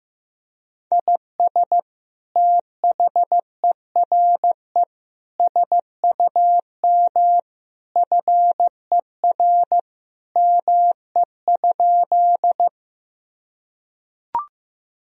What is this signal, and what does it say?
Morse code, heard as a keyed tone, about 15 wpm, 700 Hz. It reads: IS THERE SUM FER ME?
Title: Sentences from Top 200 Words and Abbreviations 15wpm